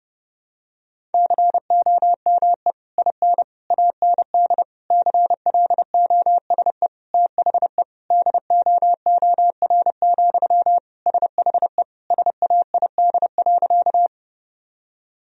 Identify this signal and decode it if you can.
Morse code, heard as a keyed tone, about 30 wpm, 700 Hz. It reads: COME IN AND CLOSE THE DOOR, SHE SAID.
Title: Sentences from Top 1000 Words - Review Part 1 - 30wpm